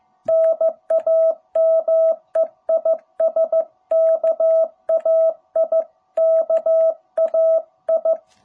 Morse code, heard as a keyed tone, about 15 wpm, 660 Hz. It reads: NAME IS KAI KAI
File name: jh8cbh / 2021cw